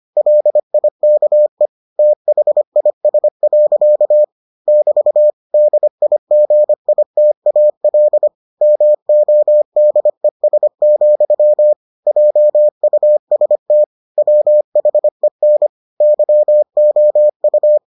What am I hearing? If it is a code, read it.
Morse code, heard as a keyed tone, about 25 wpm, 600 Hz. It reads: LIKE THIS. <BT> DIGITAL MODES, JUST WHEN YOU